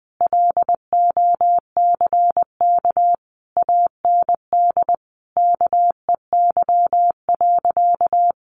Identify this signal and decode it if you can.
Morse code, heard as a keyed tone, about 20 wpm, 700 Hz. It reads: LOCK AND KEY.
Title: Sentences from Top 1400 Words 20wpm